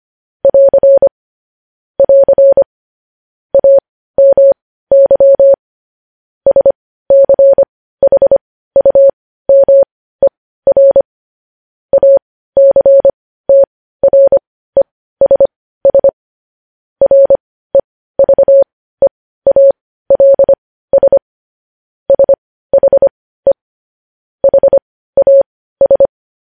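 Morse code, heard as a keyed tone, about 25 wpm, spaced as for 15 wpm, 550 Hz. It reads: <AR> <AR> AMY SCHUMER ACTRESS REVEALS SHE HAS
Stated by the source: Morse practice sending news headlines at 15 wpm.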